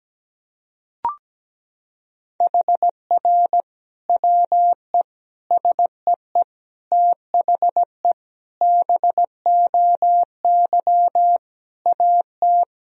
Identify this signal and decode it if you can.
Morse code, heard as a keyed tone, about 17 wpm, 700 Hz. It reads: HR WE SEE THE BOY AT
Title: Sentences from Top 300 Words and Abbreviations 17wpm